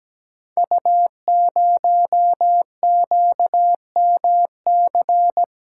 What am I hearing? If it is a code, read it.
Morse code, heard as a keyed tone, about 17 wpm, 700 Hz. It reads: U0QMC
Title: Code Groups - 5 Letter:Numbers - 17wpm